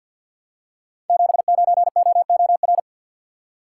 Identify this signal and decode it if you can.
Morse code, heard as a keyed tone, about 46 wpm, 700 Hz. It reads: 79OOR
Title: Code Groups - 5 Letter:Numbers - 50wpm